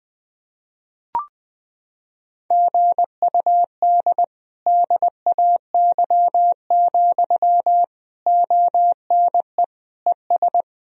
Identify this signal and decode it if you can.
Morse code, heard as a keyed tone, about 20 wpm, 700 Hz. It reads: GUD DAY, ONE ES
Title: Sentences from Top 200 Words and Abbreviations 20wpm